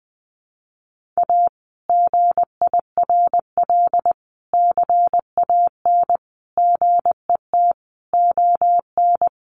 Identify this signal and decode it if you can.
Morse code, heard as a keyed tone, about 20 wpm, 700 Hz. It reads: A GIRL CAN GET ON
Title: Sentences from Top 1400 Words 20wpm